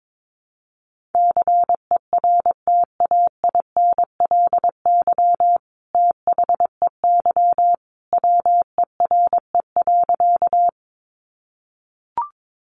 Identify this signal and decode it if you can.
Morse code, heard as a keyed tone, about 22 wpm, 700 Hz. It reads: CERTAINLY THEY WERE.
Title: Sentences from Top 900 Words 22wpm